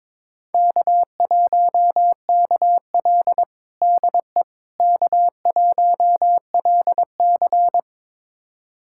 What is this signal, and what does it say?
Morse code, heard as a keyed tone, about 22 wpm, 700 Hz. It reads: K1KL DE K1LC
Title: USA Callsign DE Callsign - 4 Characters - 22wpm